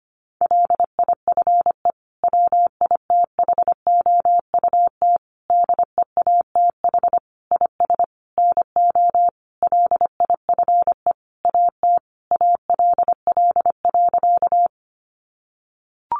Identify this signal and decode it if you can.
Morse code, heard as a keyed tone, about 25 wpm, 700 Hz. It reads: LIFE WITHOUT DEATH IS NO LIFE AT ALL.
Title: Sentences from Top 1100 Words 25wpm